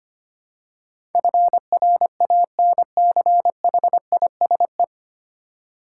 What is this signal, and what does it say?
Morse code, heard as a keyed tone, about 25 wpm, 700 Hz. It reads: FRANCHISE
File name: Nine Letter Words - 25wpm